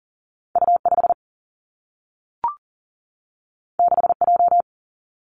Morse code, heard as a keyed tone, about 39 wpm, 700 Hz. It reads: U5 6J
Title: Two Challenging Letter-Number Combinations 40wpm